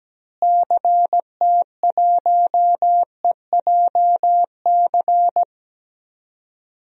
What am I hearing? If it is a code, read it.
Morse code, heard as a keyed tone, about 17 wpm, 700 Hz. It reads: CT1EJC
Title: DXCC-100 Call Signs - 6 Characters - Repeated 2x - 17wpm